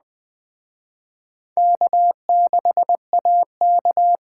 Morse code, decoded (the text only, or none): K6AK